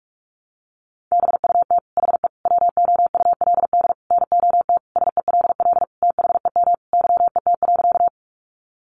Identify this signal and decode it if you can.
Morse code, heard as a keyed tone, about 44 wpm, 700 Hz. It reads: BUT HE WOULD NOT SELL THEM YET.
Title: Sentences from Top 1100 Words 45wpm